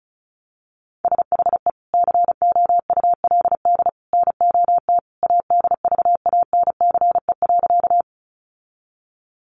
Morse code, heard as a keyed tone, about 34 wpm, 700 Hz. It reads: SHE COULD NOT ADVANCE.
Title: Sentences from Top 1400 Words 35wpm